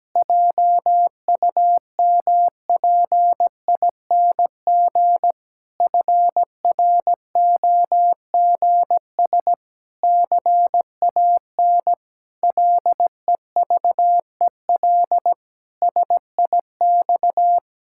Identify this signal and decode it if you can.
Morse code, heard as a keyed tone, about 17 wpm, 700 Hz. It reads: JUMPING FROGS CAN LEVEL SIX
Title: Pangram Challenge v2 - 17wpm